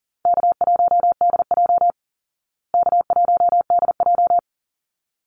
Morse code, heard as a keyed tone, about 40 wpm, 700 Hz. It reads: K1DJ K1DJ
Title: Easy Call Signs Repeated 2x - 40wpm